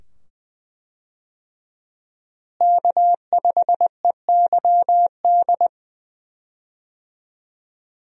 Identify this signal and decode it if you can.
Morse code, heard as a keyed tone, about 21 wpm, 700 Hz. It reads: K5EYD
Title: USA Call Signs - 5 Characters - Repeated 2x - 20wpm